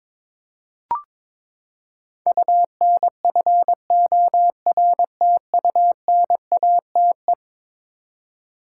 Morse code, heard as a keyed tone, about 22 wpm, 700 Hz. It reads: UNFORTUNATE
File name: practice / 22wpm / Eleven Letter Words - 22wpm